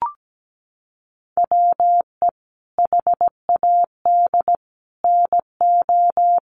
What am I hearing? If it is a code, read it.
Morse code, heard as a keyed tone, about 17 wpm, 700 Hz. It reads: WE HAD NO
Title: Sentences from Top 1100 Words 17wpm